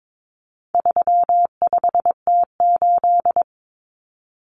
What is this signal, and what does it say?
Morse code, heard as a keyed tone, about 22 wpm, 700 Hz. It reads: 35T8
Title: Common Cut Numbers - 4-Digit 22wpm